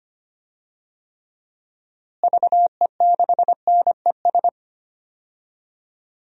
Morse code, heard as a keyed tone, about 25 wpm, 700 Hz. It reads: VE6NES
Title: DXCC-100 Call Signs - 6 Characters - Repeated 3x - 25wpm